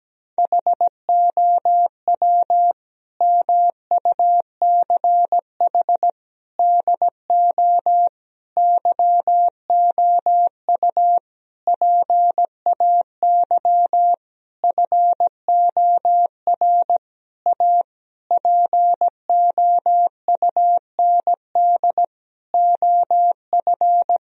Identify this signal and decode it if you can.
Morse code, heard as a keyed tone, about 17 wpm, 700 Hz. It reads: HOW MUCH DO YOU PAY FOR A POUND OF
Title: Sentences from Top 1000 Words - Review Part 3 - 17wpm